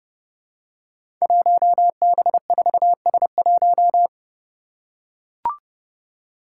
Morse code, heard as a keyed tone, about 30 wpm, 700 Hz. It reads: 1B4S1
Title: Five Challenging Letter-Number Combinations 30wpm